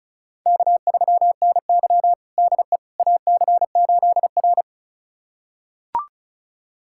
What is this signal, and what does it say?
Morse code, heard as a keyed tone, about 36 wpm, 700 Hz. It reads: K3NY DE AC8R
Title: USA Callsign DE Callsign - 4 Characters - 35wpm